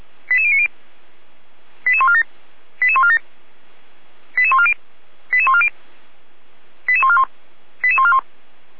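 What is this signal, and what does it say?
Five-tone selective call (ZVEI), heard as quick runs of five tones, tones of about 80 ms. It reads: 88090 88147 88147 88140 88140 88141 88141
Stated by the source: ZVEI five-tone call received by SDR, archived by fire brigade Feuerwehr Orlamünde.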